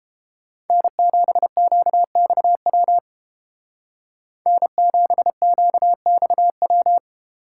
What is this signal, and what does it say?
Morse code, heard as a keyed tone, about 31 wpm, 700 Hz. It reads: N7QXW N7QXW